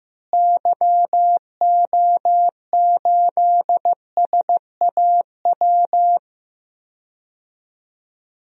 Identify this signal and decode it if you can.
Morse code, heard as a keyed tone, about 15 wpm, 700 Hz. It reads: YO8SAW
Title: DXCC-100 Call Signs - 6 Characters - Repeated 3x - 15wpm